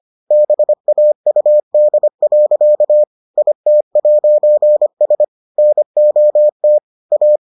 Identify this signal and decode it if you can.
Morse code, heard as a keyed tone, about 25 wpm, 600 Hz. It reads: BAUD. IT'S NOT A